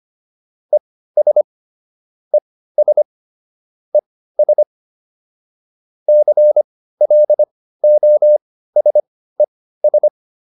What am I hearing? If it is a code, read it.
Morse code, heard as a keyed tone, about 25 wpm, spaced as for 15 wpm, 600 Hz. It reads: ES ES ES CLOSES